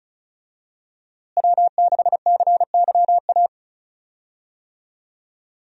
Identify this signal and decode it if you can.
Morse code, heard as a keyed tone, about 34 wpm, 700 Hz. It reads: W6CYA
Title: USA Call Signs - 5 Characters - Repeated 2x - 35wpm